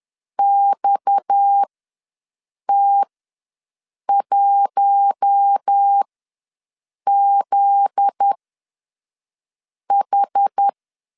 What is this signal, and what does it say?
Morse code, heard as a keyed tone, about 10 wpm, spaced as for 5 wpm, 800 Hz. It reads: XT1ZH
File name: LEZIONI / 7 GF9Z / 7 GF9Z ws 10wpm 3919